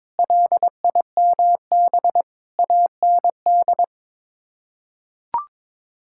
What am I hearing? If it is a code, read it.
Morse code, heard as a keyed tone, about 22 wpm, 700 Hz. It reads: LIMB AND
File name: practice / 22wpm / Sets of 8 Words - 22wpm